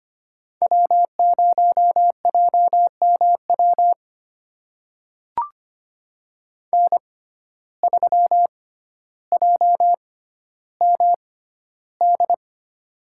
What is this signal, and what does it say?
Morse code, heard as a keyed tone, about 25 wpm, spaced as for 13 wpm, 700 Hz. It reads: W0JMW N3JMD
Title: USA Call Signs - 5 Characters - 6x Character Spacing - 25wpm